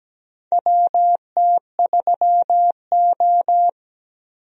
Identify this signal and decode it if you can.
Morse code, heard as a keyed tone, about 17 wpm, 700 Hz. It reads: WT3O
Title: USA Call Signs - 4 Characters - Repeated 2x - 17wpm